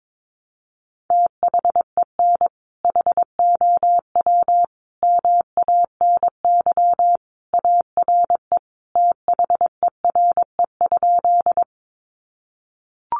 Morse code, heard as a keyed tone, about 22 wpm, 700 Hz. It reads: THEN HOW MANY ARE THERE?
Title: Sentences from Top 100 Words 22wpm